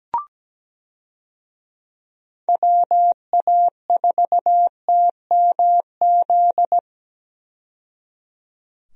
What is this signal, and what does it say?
Morse code, heard as a keyed tone, about 17 wpm, 700 Hz. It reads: WA4TMZ
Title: USA Call Signs - 6 Characters - 17wpm